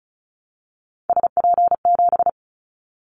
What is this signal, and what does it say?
Morse code, heard as a keyed tone, about 34 wpm, 700 Hz. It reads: SP7
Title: Callsign Prefixes for Top 25 DXCC - 35wpm